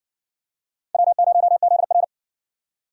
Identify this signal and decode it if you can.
Morse code, heard as a keyed tone, about 60 wpm, 700 Hz. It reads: W0ZR